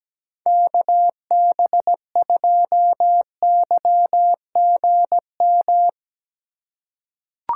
Morse code, heard as a keyed tone, about 17 wpm, 700 Hz. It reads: KB2YGM
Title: USA Call Signs - 6 Characters - 6x Character Spacing - 17wpm